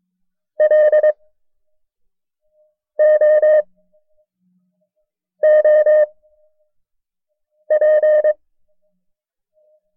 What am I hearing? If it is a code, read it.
Morse code, heard as a keyed tone, about 22 wpm, spaced as for 5 wpm, 610 Hz. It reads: LOOP
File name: FreeFiles / CW-practice / KB6MT-tape1-side2